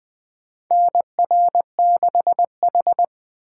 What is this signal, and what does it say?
Morse code, heard as a keyed tone, about 20 wpm, 700 Hz. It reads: NR6H
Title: USA Call Signs - 4 Characters - 6x Character Spacing - 20wpm